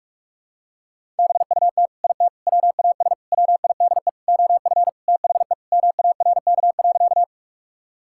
Morse code, heard as a keyed tone, about 44 wpm, 700 Hz. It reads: BUT IT WAS WIDE OF THE MARK.
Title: Sentences from Top 1000 Words - Review Part 6 - 45wpm